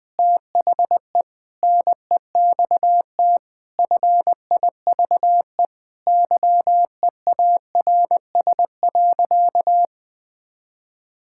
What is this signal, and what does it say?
Morse code, heard as a keyed tone, about 20 wpm, 700 Hz. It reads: THE NEXT FIVE YEARS.
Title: Sentences with 5-Word Phrases - 20wpm